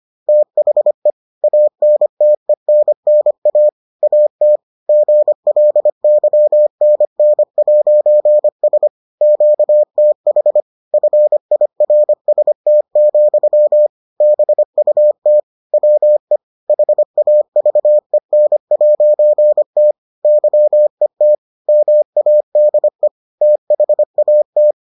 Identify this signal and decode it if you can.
Morse code, heard as a keyed tone, about 25 wpm, 600 Hz. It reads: THE ANTENNA AT GLYNN'S QTH FIRST, BUT WE HAVEN'T YET MADE THAT